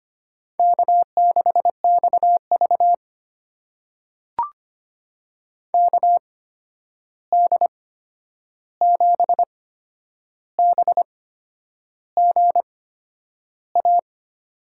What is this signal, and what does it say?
Morse code, heard as a keyed tone, about 25 wpm, spaced as for 9 wpm, 700 Hz. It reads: K6XV KD7BGA
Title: USA Call Signs - 8x Character Spacing - 25wpm